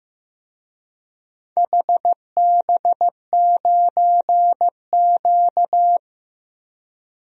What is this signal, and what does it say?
Morse code, heard as a keyed tone, about 15 wpm, 700 Hz. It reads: HB9Q
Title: DXCC-100 Call Signs - 4 Characters - Repeated 2x - 15wpm